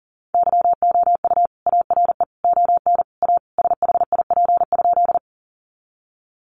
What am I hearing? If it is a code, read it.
Morse code, heard as a keyed tone, about 40 wpm, 700 Hz. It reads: YOU ARE ON A SHIP?
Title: Sentences from Top 400 Words 40wpm